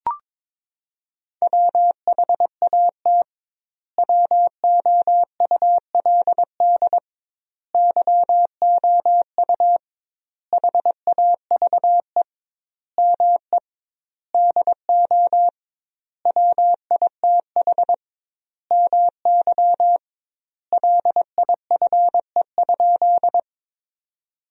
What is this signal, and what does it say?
Morse code, heard as a keyed tone, about 21 wpm, 700 Hz. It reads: WHAT WOULD YOU HAVE ME DO WITH MY LIFE?
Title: 2x Word-Spacing - Sentences from Top 300 Words - 22wpm